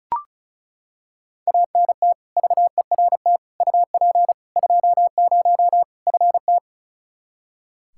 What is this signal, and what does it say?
Morse code, heard as a keyed tone, about 36 wpm, 700 Hz. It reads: ANT VERT UP 20 FT
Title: K1USN SST QSO Practice 35wpm